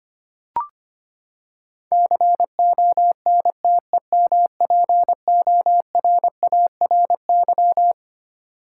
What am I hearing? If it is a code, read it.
Morse code, heard as a keyed tone, about 25 wpm, 700 Hz. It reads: CONTEMPORARY
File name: Twelve Letter Words - 25wpm